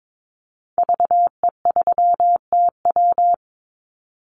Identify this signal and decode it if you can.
Morse code, heard as a keyed tone, about 22 wpm, 700 Hz. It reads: VE3TW
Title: Call Signs Repeated 3x - 22wpm